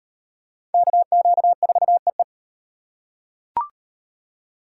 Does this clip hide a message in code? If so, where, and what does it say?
Yes: Morse code from 0.7 to 2.2 s, heard as a keyed tone, about 38 wpm, 700 Hz. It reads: KQ4EE